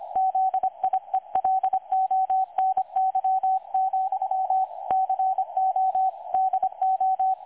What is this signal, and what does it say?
Morse code, heard as a keyed tone, about 25 wpm, 760 Hz, with radio noise. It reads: ZIELONY, CO DO
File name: dlazaawansowanych / nagraniazpasma / websdr recording 3565.0kHz